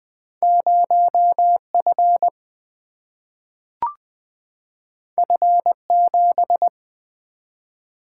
Morse code, heard as a keyed tone, about 20 wpm, 700 Hz. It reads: 0 F F 7
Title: Single-Letter Single-Number 20wpm